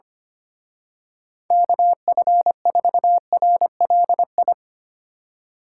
Morse code, heard as a keyed tone, about 26 wpm, 700 Hz. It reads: KF4RLI
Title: Speed Racing - USA Call Signs 25wpm